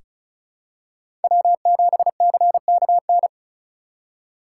Morse code, heard as a keyed tone, about 36 wpm, 700 Hz. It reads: W7CKN
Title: USA Call Signs - 5 Characters - 4x Character Spacing - 35wpm